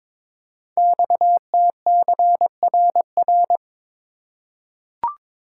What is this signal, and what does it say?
Morse code, heard as a keyed tone, about 22 wpm, 700 Hz. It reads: XTCRR
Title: Code Groups - 5 Letters - 22wpm